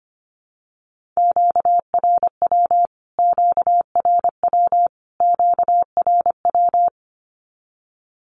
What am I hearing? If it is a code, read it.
Morse code, heard as a keyed tone, about 25 wpm, 700 Hz. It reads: QRW QRW QRW
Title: Q-Codes Repeated 3x 25wpm